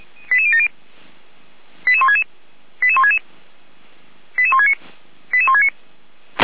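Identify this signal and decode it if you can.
Five-tone selective call (ZVEI), heard as quick runs of five tones, tones of about 70 ms. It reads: 88070 88166 88166 88169 88169
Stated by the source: ZVEI five-tone call 88090 received by SDR.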